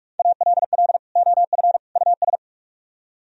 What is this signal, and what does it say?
Morse code, heard as a keyed tone, about 46 wpm, 700 Hz. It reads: ALL OF US